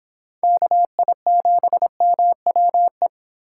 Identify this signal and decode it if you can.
Morse code, heard as a keyed tone, about 24 wpm, 700 Hz. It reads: KI7MWE